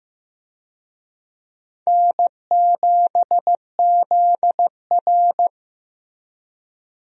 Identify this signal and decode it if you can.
Morse code, heard as a keyed tone, about 15 wpm, 700 Hz. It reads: N7ZR
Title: USA Call Signs - 4 Characters - Repeated 2x - 15wpm